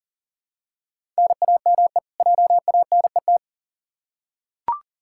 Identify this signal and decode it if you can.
Morse code, heard as a keyed tone, about 38 wpm, 700 Hz. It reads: NAME JANET
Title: K1USN SST QSO Practice 40wpm